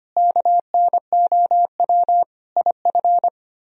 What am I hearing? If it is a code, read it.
Morse code, heard as a keyed tone, about 31 wpm, 700 Hz. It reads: KNOW IF
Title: Speed-Racing - Sets of 5 Words 25wpm